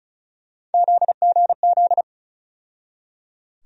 Morse code, heard as a keyed tone, about 36 wpm, 700 Hz. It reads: ZGZ
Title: USA Call Sign Postfixes - 3 Letters 35wpm